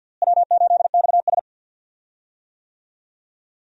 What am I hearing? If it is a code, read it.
Morse code, heard as a keyed tone, about 52 wpm, 700 Hz. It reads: W8XS
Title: USA Call Signs - 4 Characters - Repeated 3x - 50wpm